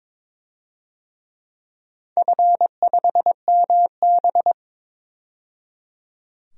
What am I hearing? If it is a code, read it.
Morse code, heard as a keyed tone, about 22 wpm, 700 Hz. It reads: F5MB
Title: DXCC-100 Call Signs - 4 Characters - 22wpm